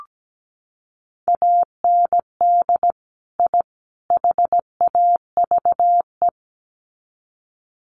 Morse code, heard as a keyed tone, about 17 wpm, 700 Hz. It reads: AND I HAVE
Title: Sets of 3 Words 17wpm